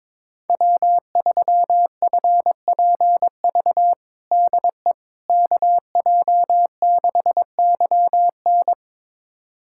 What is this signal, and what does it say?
Morse code, heard as a keyed tone, about 22 wpm, 700 Hz. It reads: W3FPV DE KJ6YN
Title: USA Callsign DE Callsign - 5 Characters - 22wpm